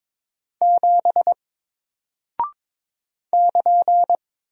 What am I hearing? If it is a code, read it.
Morse code, heard as a keyed tone, about 22 wpm, 700 Hz. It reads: 7 <KN>
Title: Single Character (Letter:Number:Punctuation:Prosign) Alt 22wpm